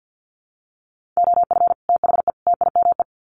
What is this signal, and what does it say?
Morse code, heard as a keyed tone, about 50 wpm, 700 Hz. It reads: OF THE TIME
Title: Sets of 3 Words 50wpm